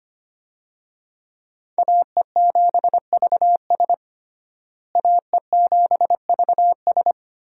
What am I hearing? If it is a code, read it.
Morse code, heard as a keyed tone, about 25 wpm, 700 Hz. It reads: AE7VS AE7VS
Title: USA Call Signs - 5 Characters - Repeated 2x - 25wpm